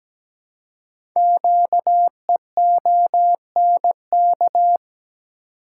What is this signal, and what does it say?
Morse code, heard as a keyed tone, about 17 wpm, 700 Hz. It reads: QEONK
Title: Code Groups - 5 Letters - 17wpm